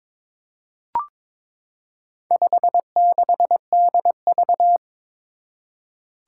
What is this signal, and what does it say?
Morse code, heard as a keyed tone, about 22 wpm, 700 Hz. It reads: 56DV
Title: Four Challenging Letter-Number Combinations 22wpm